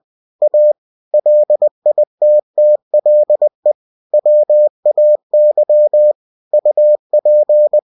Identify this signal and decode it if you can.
Morse code, heard as a keyed tone, about 20 wpm, 600 Hz. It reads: A LITTLE WAY UP
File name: cw-books / winnie-the-pooh / winnie-20wpm-page-0084